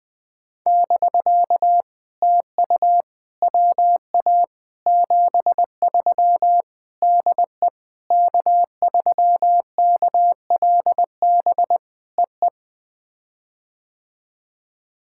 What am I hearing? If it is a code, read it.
Morse code, heard as a keyed tone, about 21 wpm, 700 Hz. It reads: <BK> TU WA 73 DE K3KLB EE
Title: POTA QSO Practice - 20wpm